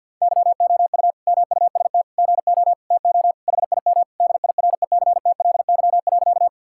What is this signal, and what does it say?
Morse code, heard as a keyed tone, about 46 wpm, 700 Hz. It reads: YOU MUST GO TO HIM DIRECTLY.
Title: Sentences from Top 1000 Words - Review Part 5 - 50wpm